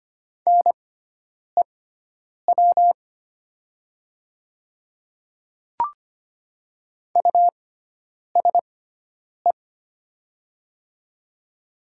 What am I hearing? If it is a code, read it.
Morse code, heard as a keyed tone, about 25 wpm, spaced as for 9 wpm, 700 Hz. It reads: NEW USE
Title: 6x Character Spacing - Top 300 Words - 25wpm